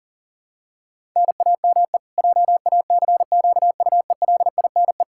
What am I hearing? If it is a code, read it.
Morse code, heard as a keyed tone, about 37 wpm, 700 Hz. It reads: NAME JACQUELINE
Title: Practice QSO - Exchange 1 - Beginning - 40wpm